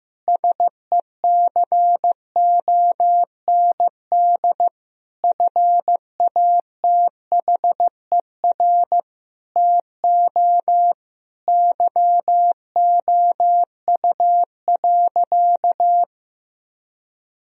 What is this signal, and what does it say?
Morse code, heard as a keyed tone, about 16 wpm, 700 Hz. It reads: SECOND FATHER TO YOU.
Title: Sentences from Top 1000 Words - Review Part 1 - 15wpm